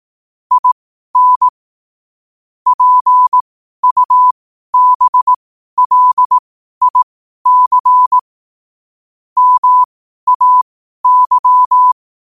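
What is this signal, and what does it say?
Morse code, heard as a keyed tone, about 18 wpm, spaced as for 12 wpm, 1000 Hz. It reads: IN PUBLIC MAY
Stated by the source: Morse code sent at 12 words per minute.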